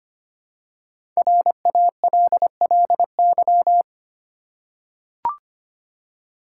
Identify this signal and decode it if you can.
Morse code, heard as a keyed tone, about 25 wpm, 700 Hz. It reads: RALLY
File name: 5 Letter Words - First 3 Letters Given - 25wpm